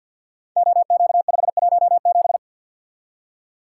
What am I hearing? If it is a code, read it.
Morse code, heard as a keyed tone, about 53 wpm, 700 Hz. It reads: OQ517
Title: Code Groups - 5 Letter:Numbers - 50wpm